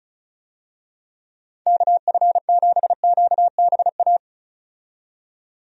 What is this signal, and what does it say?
Morse code, heard as a keyed tone, about 36 wpm, 700 Hz. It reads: KF7QBA
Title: USA Call Signs - 6 Characters - 35wpm